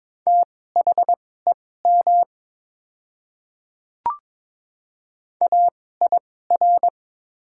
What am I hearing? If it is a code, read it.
Morse code, heard as a keyed tone, about 23 wpm, spaced as for 16 wpm, 690 Hz. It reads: THEM AIR
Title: 2x Character Spacing - Top 200 Words - 22wpm